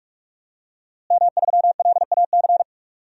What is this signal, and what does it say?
Morse code, heard as a keyed tone, about 43 wpm, 700 Hz. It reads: M3LAC
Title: DXCC-100 Call Signs - 5 Characters - Repeated 3x - 45wpm